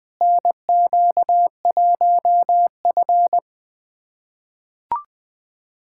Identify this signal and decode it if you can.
Morse code, heard as a keyed tone, about 20 wpm, 700 Hz. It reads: NQ1F
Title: USA Callsign DE Callsign - 4 Characters - 20wpm